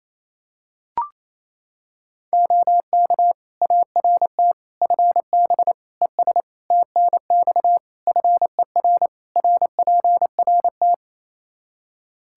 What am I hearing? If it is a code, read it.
Morse code, heard as a keyed tone, about 28 wpm, 700 Hz. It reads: OK ART FB ES TNX FER RPRT
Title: Practice QSO - Exchange 2 - Info - 28wpm